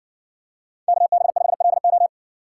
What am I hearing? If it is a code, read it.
Morse code, heard as a keyed tone, about 59 wpm, 700 Hz. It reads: KB5LO